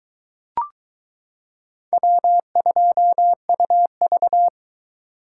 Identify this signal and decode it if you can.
Morse code, heard as a keyed tone, about 22 wpm, 700 Hz. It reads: W2UV